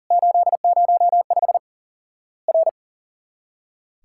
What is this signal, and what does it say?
Morse code, heard as a keyed tone, about 40 wpm, 700 Hz. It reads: 805 R
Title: Worked All Europe DX Contest - QTC Exchange - 40wpm